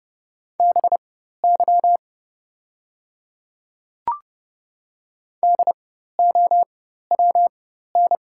Morse code, heard as a keyed tone, about 30 wpm, spaced as for 14 wpm, 700 Hz. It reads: BY DOWN
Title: 4x Character Spacing - Top 100 Words - 30wpm